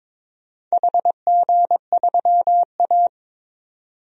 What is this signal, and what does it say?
Morse code, heard as a keyed tone, about 23 wpm, 700 Hz. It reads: HG3A
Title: DXCC-100 Call Signs - 4 Characters - 22wpm